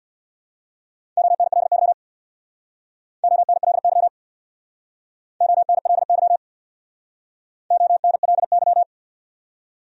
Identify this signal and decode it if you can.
Morse code, heard as a keyed tone, about 62 wpm, 700 Hz. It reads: ONLY ONLY ONLY ONLY
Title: Speed-Racing Top 500 Words 50wpm